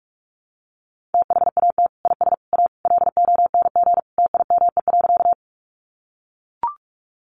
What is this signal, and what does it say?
Morse code, heard as a keyed tone, about 44 wpm, 700 Hz. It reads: THAT IS A LONG TIME.
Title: Sentences from Top 100 Words 45wpm